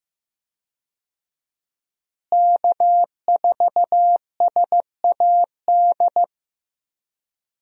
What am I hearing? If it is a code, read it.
Morse code, heard as a keyed tone, about 15 wpm, 700 Hz. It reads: K4SAD
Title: USA Call Signs - 5 Characters - Repeated 3x - 15wpm